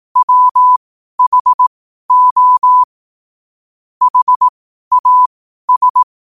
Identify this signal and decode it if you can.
Morse code, heard as a keyed tone, about 18 wpm, spaced as for 12 wpm, 1000 Hz. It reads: WHO HAS
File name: qotdfiles / 12wpm